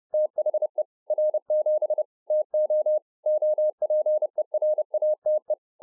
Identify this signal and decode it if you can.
Morse code, heard as a keyed tone, about 30 wpm, 610 Hz. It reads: THE R7 TO OPERATE